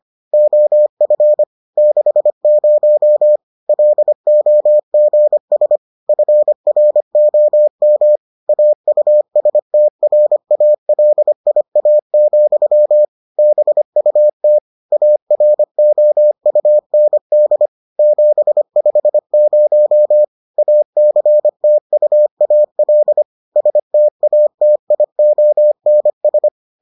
Morse code, heard as a keyed tone, about 25 wpm, 600 Hz. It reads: OF 60 LOGS FROM AUSTRALIA, BUT AROUND 750 ACTUAL STATIONS